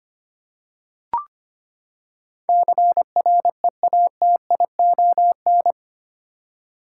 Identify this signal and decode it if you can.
Morse code, heard as a keyed tone, about 25 wpm, 700 Hz. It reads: CREATION
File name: Eight Letter Words - 25wpm